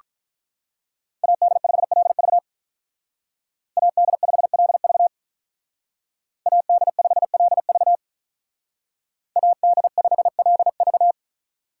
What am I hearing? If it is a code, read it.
Morse code, heard as a keyed tone, about 43 wpm, 700 Hz. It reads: AD5LV AD5LV AD5LV AD5LV
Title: Speed Racing - USA Call Signs 35wpm